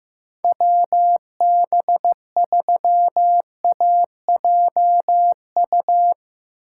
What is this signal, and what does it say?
Morse code, heard as a keyed tone, about 15 wpm, 700 Hz. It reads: WB3AJU
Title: Speed Racing - USA 2x3 Call Signs 15wpm